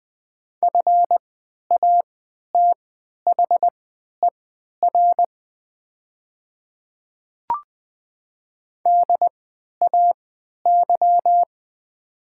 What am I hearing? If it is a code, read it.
Morse code, heard as a keyed tone, about 20 wpm, spaced as for 11 wpm, 700 Hz. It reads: FATHER DAY
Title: 3x Character Spacing - Top 300 Words - 20wpm